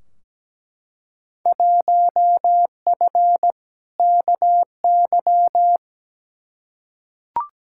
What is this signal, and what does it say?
Morse code, heard as a keyed tone, about 17 wpm, 700 Hz. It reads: 1F KY
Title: Field Day Exchanges Encore - 17wpm